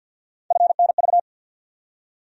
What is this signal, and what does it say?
Morse code, heard as a keyed tone, about 54 wpm, 700 Hz. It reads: FNV
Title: USA Call Sign Postfixes - 3 Letters 50wpm